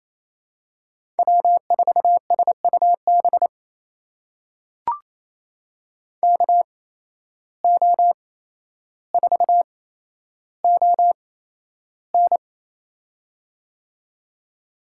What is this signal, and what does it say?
Morse code, heard as a keyed tone, about 28 wpm, spaced as for 12 wpm, 700 Hz. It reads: W4SUB KO4ON
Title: USA Call Signs - 5 Characters - 8x Character Spacing - 28wpm